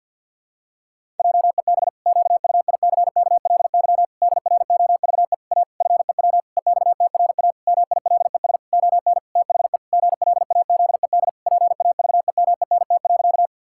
Morse code, heard as a keyed tone, about 51 wpm, 700 Hz. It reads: JEB QUICKLY DROVE A FEW EXTRA MILES ON THE GLAZED PAVEMENT.
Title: Pangram Challenge v2 - 50wpm